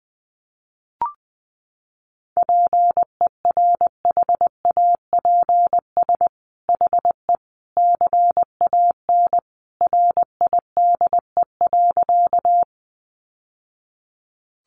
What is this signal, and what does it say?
Morse code, heard as a keyed tone, about 20 wpm, 700 Hz. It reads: PERHAPS HE CAN RIDE.
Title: Sentences from Top 1200 Words 20wpm